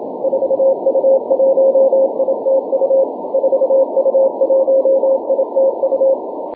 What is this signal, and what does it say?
Morse code, heard as a keyed tone, about 27 wpm, 540 Hz. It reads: 4U1ITU 4U1ITU
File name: wp / 4u1itu 6m cw may 26th 2011